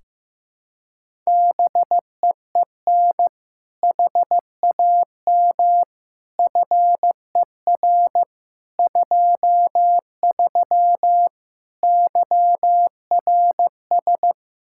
Morse code, heard as a keyed tone, about 16 wpm, 700 Hz. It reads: BEEN HAM FER 23 YRS
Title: Practice QSO - Exchange 3 - More Info - 15wpm